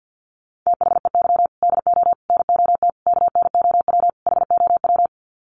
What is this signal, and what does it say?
Morse code, heard as a keyed tone, about 50 wpm, 700 Hz. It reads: THEY DO NOT KNOW HOW
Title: Sets of 5 Words Encore2 50wpm